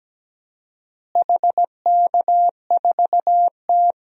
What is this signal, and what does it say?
Morse code, heard as a keyed tone, about 17 wpm, 700 Hz. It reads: HK4T
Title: DXCC-100 Call Signs - 4 Characters - Repeated 2x - 17wpm